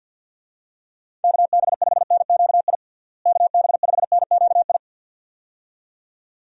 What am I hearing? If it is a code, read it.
Morse code, heard as a keyed tone, about 51 wpm, 700 Hz. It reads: KB5NQI KB5NQI
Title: USA Call Signs - 6 Characters - Repeated 2x - 50wpm